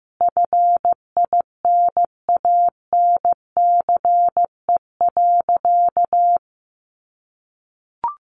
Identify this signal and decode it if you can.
Morse code, heard as a keyed tone, about 15 wpm, 700 Hz. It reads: FINANCE.
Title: Sentences from Top 1500 Words 15wpm